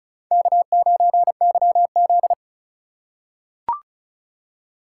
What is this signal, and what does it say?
Morse code, heard as a keyed tone, about 34 wpm, 700 Hz. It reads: K9YZ
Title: USA Call Signs - 4 Characters - 4x Character Spacing - 35wpm